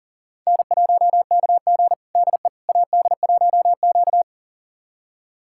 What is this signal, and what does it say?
Morse code, heard as a keyed tone, about 37 wpm, 700 Hz. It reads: N1KG DE AD1Q
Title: USA Callsign DE Callsign - 4 Characters - 40wpm